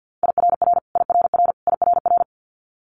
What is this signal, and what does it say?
Morse code, heard as a keyed tone, about 50 wpm, 700 Hz. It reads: IRR IRR IRR
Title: Multi-letter Phonemes Repeated 3x 50wpm